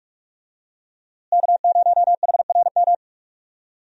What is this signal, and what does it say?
Morse code, heard as a keyed tone, about 43 wpm, 700 Hz. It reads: K0HRM
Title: USA Call Signs - 5 Characters - Repeated 2x - 45wpm